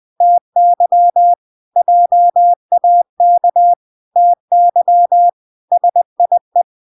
Morse code, heard as a keyed tone, about 20 wpm, 700 Hz. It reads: TY JAK TY SIE